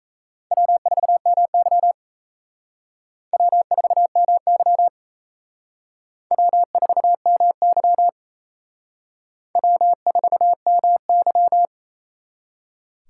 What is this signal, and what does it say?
Morse code, heard as a keyed tone, about 35 wpm, 700 Hz. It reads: W4MY W4MY W4MY W4MY
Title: Speed Racing - USA 1x2 Call Signs 28wpm